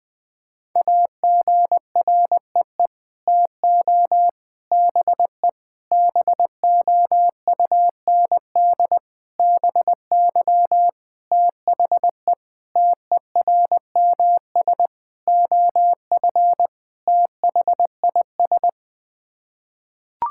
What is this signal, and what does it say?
Morse code, heard as a keyed tone, about 21 wpm, 700 Hz. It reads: AGREE TO BE BOUND BY THE TERMS OF THIS
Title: Sets of 9 Words - 20wpm